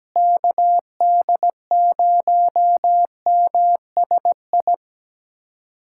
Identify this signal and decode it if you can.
Morse code, heard as a keyed tone, about 17 wpm, 700 Hz. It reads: KD0MSI
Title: USA Callsign DE Callsign - 6 Characters - 17wpm